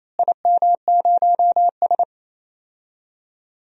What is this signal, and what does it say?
Morse code, heard as a keyed tone, about 28 wpm, 700 Hz. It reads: IM0S
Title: DXCC-100 Call Signs - 4 Characters - Repeated 2x - 28wpm